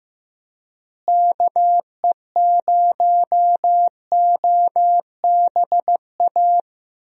Morse code, heard as a keyed tone, about 15 wpm, 700 Hz. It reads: KE0OBA
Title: USA Call Signs - 6 Characters - Repeated 2x - 15wpm